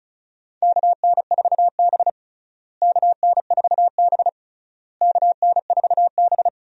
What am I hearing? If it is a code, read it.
Morse code, heard as a keyed tone, about 36 wpm, 700 Hz. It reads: KN4B KN4B KN4B
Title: USA Call Signs - 4 Characters - Repeated 3x - 35wpm